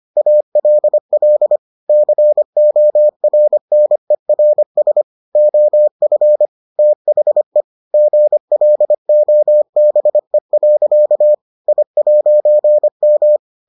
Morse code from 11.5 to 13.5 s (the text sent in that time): I'M